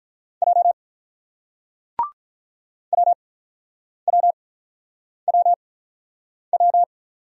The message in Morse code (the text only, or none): J WWWW